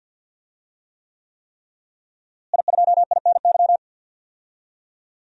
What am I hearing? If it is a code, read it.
Morse code, heard as a keyed tone, about 51 wpm, 700 Hz. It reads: I2INY
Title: DXCC-100 Call Signs - 5 Characters - Repeated 3x - 50wpm